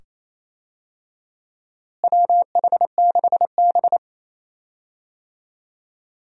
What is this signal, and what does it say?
Morse code, heard as a keyed tone, about 28 wpm, 700 Hz. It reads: WH6B
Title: USA Call Signs - 4 Characters - Repeated 2x - 28wpm